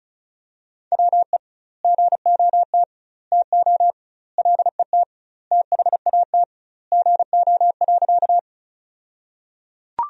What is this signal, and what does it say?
Morse code, heard as a keyed tone, about 34 wpm, 700 Hz. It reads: WE GOT TO LET THAT GO.
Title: 2x Word-Spacing - Sentences from Top 300 Words - 35wpm